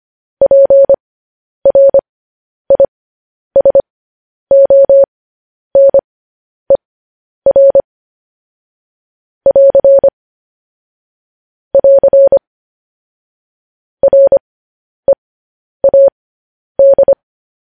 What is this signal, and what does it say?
Morse code, heard as a keyed tone, about 25 wpm, spaced as for 10 wpm, 550 Hz. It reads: PRISONER <AR> <AR> READ
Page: news-headlines-in-morse-code-at-10-wpm